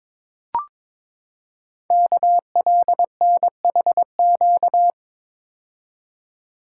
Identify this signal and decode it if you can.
Morse code, heard as a keyed tone, about 22 wpm, 700 Hz. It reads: KLNHQ
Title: Code Groups - 5 Letters - 22wpm